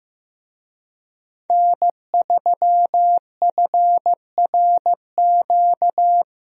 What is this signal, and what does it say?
Morse code, heard as a keyed tone, about 15 wpm, 700 Hz. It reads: N3FRQ
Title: USA Call Signs - 5 Characters - 15wpm